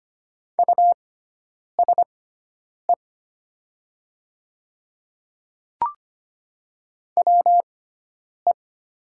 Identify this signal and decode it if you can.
Morse code, heard as a keyed tone, about 25 wpm, spaced as for 9 wpm, 700 Hz. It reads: USE WE
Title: 6x Character Spacing - Top 100 Words - 25wpm